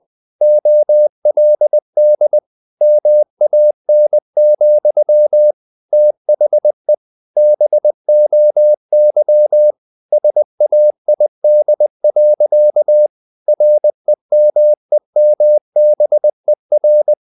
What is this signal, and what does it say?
Morse code, heard as a keyed tone, about 20 wpm, 600 Hz. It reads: OLD MAN, THE BOY SAID. REMEMBER
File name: the old man and the sea / the old man and the sea part 1 20wpm